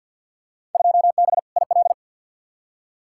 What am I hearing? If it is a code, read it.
Morse code, heard as a keyed tone, about 47 wpm, 700 Hz. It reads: 2B IL
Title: Field Day Exchanges Encore - 50wpm